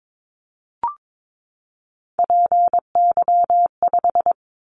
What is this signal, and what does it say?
Morse code, heard as a keyed tone, about 22 wpm, 700 Hz. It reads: PY5
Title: Callsign Prefixes for Top 100 DXCC - 22wpm